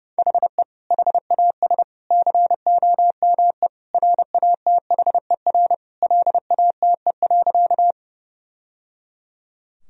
Morse code, heard as a keyed tone, about 30 wpm, 700 Hz. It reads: HE HAS COME RATHER LATE.
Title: Sentences from Top 1000 Words - Review Part 4 - 30wpm